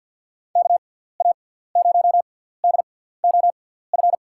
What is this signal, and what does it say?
Morse code, heard as a keyed tone, about 52 wpm, spaced as for 17 wpm, 700 Hz. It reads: KA0DOF
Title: USA Call Signs - 6 Characters - 6x Character Spacing - 50wpm